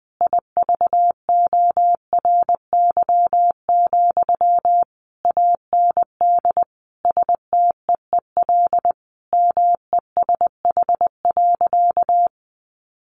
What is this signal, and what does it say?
Morse code, heard as a keyed tone, about 20 wpm, 700 Hz. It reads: IVORY, AND STEEL MESH.
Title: A Sound Of Thunder 20wpm